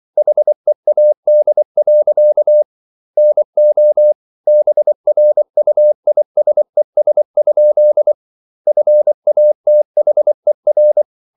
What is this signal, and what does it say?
Morse code, heard as a keyed tone, about 24 wpm, 600 Hz. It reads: HEAD. NO BRUISES? FATHER